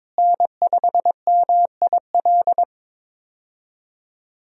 Morse code, heard as a keyed tone, about 22 wpm, 700 Hz. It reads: N5MIL
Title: USA Call Signs - 5 Characters - Repeated 3x - 22wpm